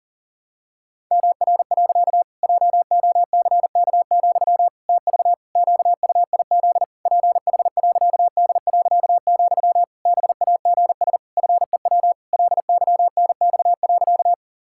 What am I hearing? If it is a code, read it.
Morse code, heard as a keyed tone, about 40 wpm, 700 Hz. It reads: MR. JOCK, TV QUIZ PH.D., BAGS FEW LYNX.
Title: Pangram Challenge v2 - 40wpm